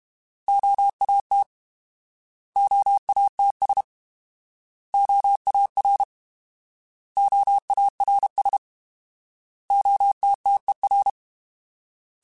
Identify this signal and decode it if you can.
Morse code, heard as a keyed tone, about 32 wpm, 800 Hz. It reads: OAT OATS OAR OARS OTTER
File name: Alphabetical / Alphabetical mp3s / Os @30